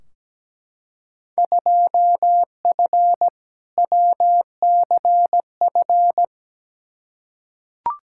Morse code, heard as a keyed tone, about 17 wpm, 700 Hz. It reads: 2F WCF
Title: Field Day Exchanges Encore - 17wpm